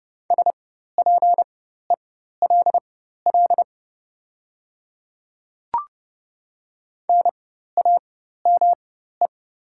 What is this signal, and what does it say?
Morse code, heard as a keyed tone, about 30 wpm, spaced as for 14 wpm, 690 Hz. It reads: SPELL NAME
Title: 4x Character Spacing - Top 500 Words - 30wpm